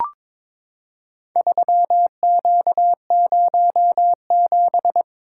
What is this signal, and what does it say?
Morse code, heard as a keyed tone, about 22 wpm, 700 Hz. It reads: 3Q07
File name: practice / 22wpm / Four Letter-Number Combinations - 22wpm